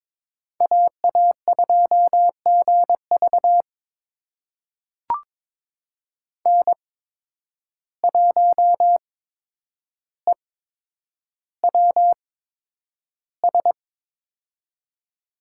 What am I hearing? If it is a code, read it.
Morse code, heard as a keyed tone, about 22 wpm, spaced as for 9 wpm, 700 Hz. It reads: AA2GV N1EWS
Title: USA Call Signs - 5 Characters - 8x Character Spacing - 22wpm